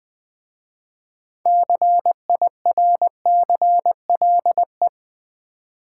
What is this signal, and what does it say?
Morse code, heard as a keyed tone, about 20 wpm, 700 Hz. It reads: CIRCLE
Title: Easy Words with Hints - 20wpm